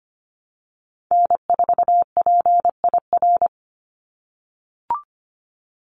Morse code, heard as a keyed tone, about 25 wpm, 700 Hz. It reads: N4PIR
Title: Call Signs - 25wpm